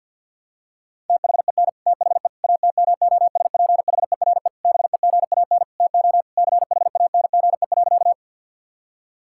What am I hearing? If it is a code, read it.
Morse code, heard as a keyed tone, about 50 wpm, 700 Hz. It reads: THEN THE ATMOSPHERE BEGAN TO CHANGE.
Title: Sentences from Top 1600 Words 50wpm